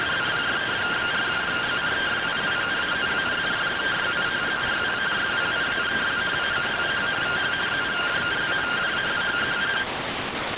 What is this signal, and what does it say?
Two-tone teleprinter signal, about 170 Hz shift.
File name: findings / MR AXIAL 33,99mm 1.gen-29 rx